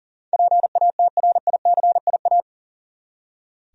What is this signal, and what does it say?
Morse code, heard as a keyed tone, about 39 wpm, 700 Hz. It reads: PATRICIA
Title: USA Common Names - 40wpm